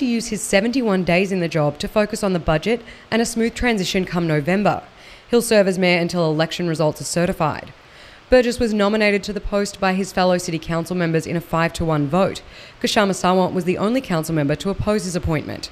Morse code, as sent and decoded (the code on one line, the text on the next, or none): none